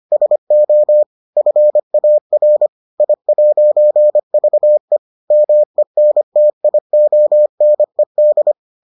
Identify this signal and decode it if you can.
Morse code, heard as a keyed tone, about 25 wpm, 600 Hz. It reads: SO FAR I'VE MENTIONED